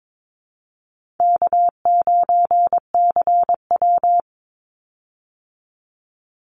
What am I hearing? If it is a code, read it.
Morse code, heard as a keyed tone, about 22 wpm, 700 Hz. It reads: K9CW
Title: Easy Call Signs - 22wpm